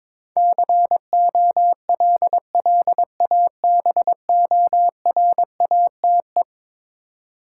Morse code, heard as a keyed tone, about 22 wpm, 700 Hz. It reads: COLLABORATE
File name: practice / 22wpm / Eleven Letter Words - 22wpm